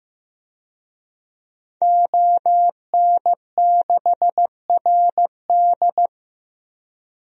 Morse code, heard as a keyed tone, about 15 wpm, 700 Hz. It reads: ON6RD
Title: DXCC-100 Call Signs - 5 Characters - 15wpm